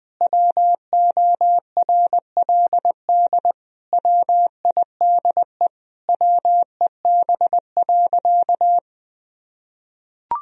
Morse code, heard as a keyed tone, about 20 wpm, 700 Hz. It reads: WORLD WIDE WEB.
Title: Sentences from Top 1000 Words - Review Part 4 - 20wpm